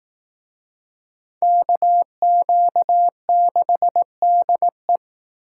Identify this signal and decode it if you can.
Morse code, heard as a keyed tone, about 18 wpm, 700 Hz. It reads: KQ6DE
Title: Speed Racing - USA 2x2 Call Signs 15wpm